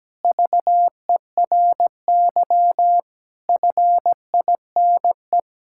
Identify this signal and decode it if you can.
Morse code, heard as a keyed tone, about 17 wpm, 700 Hz. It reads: VERY FINE
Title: Sentences from Top 1000 Words - Review Part 6 - 17wpm